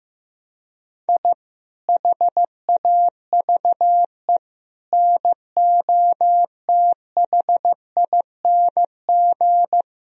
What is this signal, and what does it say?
Morse code, heard as a keyed tone, about 14 wpm, 700 Hz. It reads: I HAVE NOTHING
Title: Sentences from Top 1000 Words - Review Part 1 - 15wpm